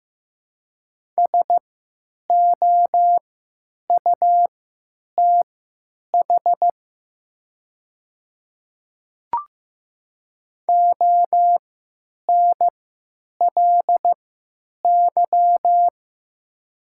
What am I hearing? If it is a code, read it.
Morse code, heard as a keyed tone, about 15 wpm, spaced as for 9 wpm, 700 Hz. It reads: SOUTH ONLY
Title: 3x Character Spacing - Top 500 Words - 15wpm